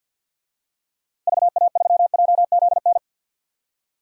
Morse code, heard as a keyed tone, about 50 wpm, 700 Hz. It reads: VA3JZN
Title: DXCC-100 Call Signs - 6 Characters - 50wpm